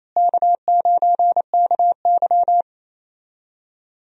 Morse code, heard as a keyed tone, about 28 wpm, 700 Hz. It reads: K9KY
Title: USA Call Signs - 4 Characters - Repeated 2x - 28wpm